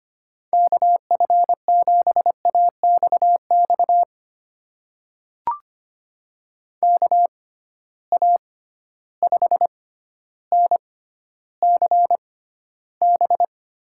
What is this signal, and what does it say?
Morse code, heard as a keyed tone, about 25 wpm, spaced as for 13 wpm, 700 Hz. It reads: KF7AXX KA5NCB
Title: USA Call Signs - 6 Characters - 6x Character Spacing - 25wpm